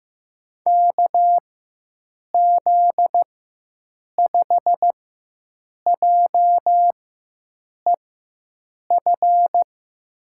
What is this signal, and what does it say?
Morse code, heard as a keyed tone, about 16 wpm, spaced as for 7 wpm, 700 Hz. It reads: KZ5JEF
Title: USA Call Signs - 6 Characters - 4x Character Spacing - 15wpm